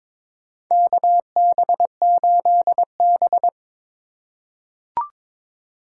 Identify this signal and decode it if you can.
Morse code, heard as a keyed tone, about 22 wpm, 700 Hz. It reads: KB8B
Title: USA Call Signs - 4 Characters - 3x Character Spacing - 22wpm